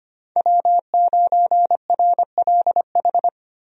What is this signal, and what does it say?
Morse code, heard as a keyed tone, about 25 wpm, 700 Hz. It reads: W9RLH
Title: USA Call Signs - 5 Characters - Repeated 2x - 25wpm